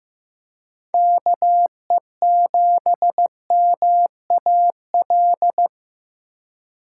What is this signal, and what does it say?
Morse code, heard as a keyed tone, about 18 wpm, 700 Hz. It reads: KE7MAL
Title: Speed Racing - USA 2x3 Call Signs 15wpm